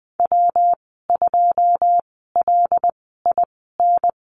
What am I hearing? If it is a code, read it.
Morse code, heard as a keyed tone, about 20 wpm, spaced as for 14 wpm, 700 Hz. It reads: W2LIN
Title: 179 Lesson 29g - Slash - Call Signs 20wpm